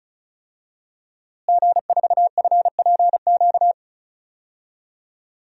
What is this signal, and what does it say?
Morse code, heard as a keyed tone, about 36 wpm, 700 Hz. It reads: G4FPQ
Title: DXCC-100 Call Signs - 5 Characters - Repeated 3x - 35wpm